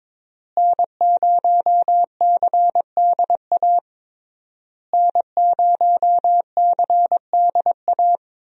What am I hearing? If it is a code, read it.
Morse code, heard as a keyed tone, about 22 wpm, 700 Hz. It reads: N0CDA N0CDA
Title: USA Call Signs - 5 Characters - Repeated 3x - 22wpm